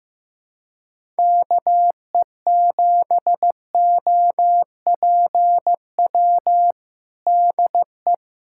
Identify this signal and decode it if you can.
Morse code, heard as a keyed tone, about 15 wpm, 700 Hz. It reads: KE7OPW DE
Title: USA Callsign DE Callsign - 6 Characters - 15wpm